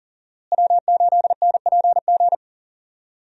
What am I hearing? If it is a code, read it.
Morse code, heard as a keyed tone, about 40 wpm, 700 Hz. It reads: W8NPG
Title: USA Call Signs - 5 Characters - Repeated 3x - 40wpm